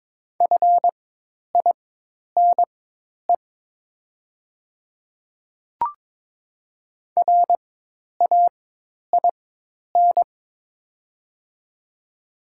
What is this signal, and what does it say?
Morse code, heard as a keyed tone, about 22 wpm, spaced as for 10 wpm, 700 Hz. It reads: FINE RAIN
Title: 4x Character Spacing - Top 500 Words - 22wpm